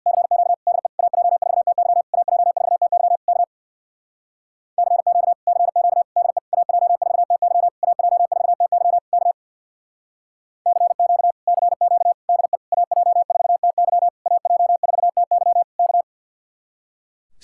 Morse code, heard as a keyed tone, about 67 wpm, 700 Hz. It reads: CQ DE AJ4TY AJ4TY K CQ CQ DE AJ4TY AJ4TY K CQ CQ DE AJ4TY AJ4TY K